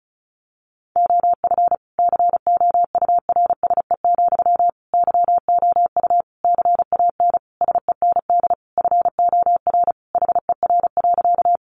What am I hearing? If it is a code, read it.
Morse code, heard as a keyed tone, about 34 wpm, 700 Hz. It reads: OF COURSE, YOU CAN SEND FOR HER.
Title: Sentences from Top 1500 Words 35wpm